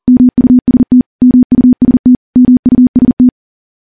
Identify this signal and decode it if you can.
Morse code, heard as a keyed tone, about 42 wpm, 250 Hz. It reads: MUST MUST MUST
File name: morse / morse-training-40WPM